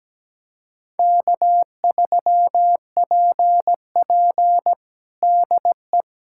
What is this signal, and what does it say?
Morse code, heard as a keyed tone, about 17 wpm, 700 Hz. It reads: K3PP DE
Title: USA Callsign DE Callsign - 4 Characters - 17wpm